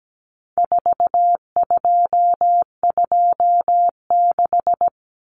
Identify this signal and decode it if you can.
Morse code, heard as a keyed tone, about 17 wpm, 700 Hz. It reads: 4226
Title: Common Cut Numbers - 4-Digit 17wpm